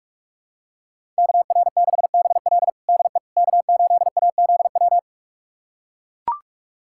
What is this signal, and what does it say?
Morse code, heard as a keyed tone, about 41 wpm, 700 Hz. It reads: KR6BL DE K8AZW
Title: USA Callsign DE Callsign - 5 Characters - 45wpm